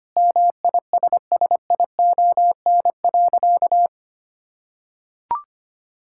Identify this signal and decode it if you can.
Morse code, heard as a keyed tone, about 25 wpm, 700 Hz. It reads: MISSION.
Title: Sentences from Top 1600 Words 25wpm